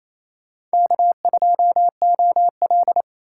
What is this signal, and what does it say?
Morse code, heard as a keyed tone, about 35 wpm, 700 Hz. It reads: K2OL
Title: Speed Racing - USA 1x2 Call Signs 28wpm